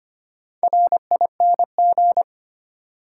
Morse code, heard as a keyed tone, about 25 wpm, 700 Hz. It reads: RING
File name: practice / 25wpm / Easy Words with Hints - 25wpm